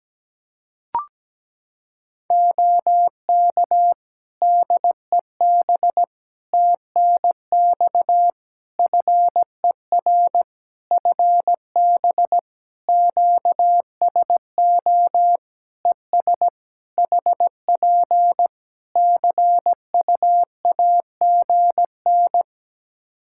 Morse code, heard as a keyed tone, about 17 wpm, 700 Hz. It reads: OK DEB TNX FER FB QSO ES HP CU AGN
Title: Practice QSO - Exchange 4 - Ending - 17wpm